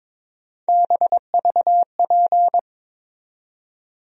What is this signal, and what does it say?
Morse code, heard as a keyed tone, about 22 wpm, 700 Hz. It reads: BVP
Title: USA Call Sign Postfixes - 3 Letters 22wpm